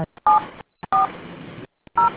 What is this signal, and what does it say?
DTMF, heard as telephone touch tones, tones of about 130 ms. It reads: 740